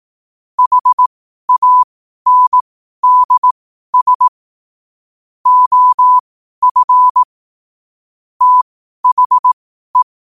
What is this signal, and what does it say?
Morse code, heard as a keyed tone, about 18 wpm, spaced as for 12 wpm, 1000 Hz. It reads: HANDS OF THE
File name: qotdfiles / 12wpm